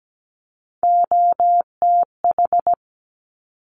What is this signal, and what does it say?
Morse code, heard as a keyed tone, about 17 wpm, 690 Hz. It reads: OTH
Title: Top Three Letter Combinations in Words 17wpm